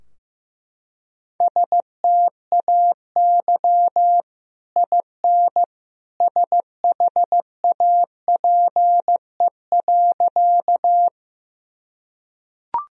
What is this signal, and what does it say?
Morse code, heard as a keyed tone, about 15 wpm, 700 Hz. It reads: STAY IN SHAPE.
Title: Sentences from Top 1000 Words - Review Part 3 - 15wpm